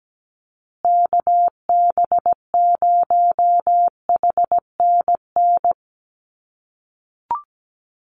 Morse code, heard as a keyed tone, about 17 wpm, 700 Hz. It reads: KB0HNN
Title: USA Call Signs Repeated 2x 17wpm